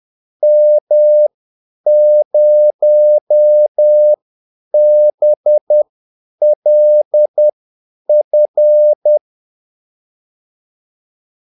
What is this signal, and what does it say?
Morse code, heard as a keyed tone, about 10 wpm, spaced as for 8 wpm, 600 Hz. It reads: M0BLF
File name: Callsigns-10-8wpm